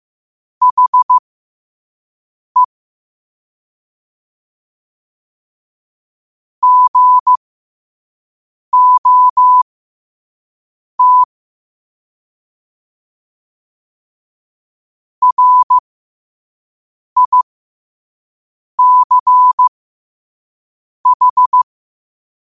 Morse code, heard as a keyed tone, about 15 wpm, spaced as for 5 wpm, 1000 Hz. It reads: HE GOT RICH
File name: qotdfiles / 5wpm